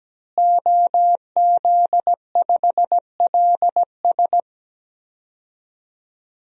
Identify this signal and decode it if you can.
Morse code, heard as a keyed tone, about 17 wpm, 700 Hz. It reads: OZ5LS
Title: DXCC-100 Call Signs - 5 Characters - Repeated 3x - 17wpm